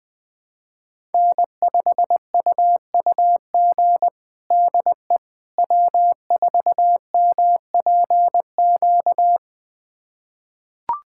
Morse code, heard as a keyed tone, about 20 wpm, 700 Hz. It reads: N5UUG DE W4MPQ
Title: USA Callsign DE Callsign - 5 Characters - 20wpm